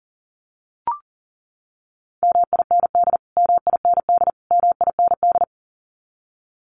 Morse code, heard as a keyed tone, about 40 wpm, 700 Hz. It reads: MIND MIND MIND
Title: Top 500 Words Repeated 40wpm